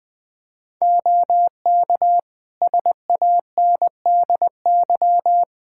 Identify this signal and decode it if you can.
Morse code, heard as a keyed tone, about 21 wpm, 700 Hz. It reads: OK SANDY
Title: Practice QSO - Exchange 3 - More Info - 20wpm